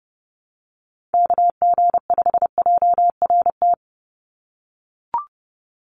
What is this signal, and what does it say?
Morse code, heard as a keyed tone, about 30 wpm, 700 Hz. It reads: KG5JRT
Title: USA Call Signs 30wpm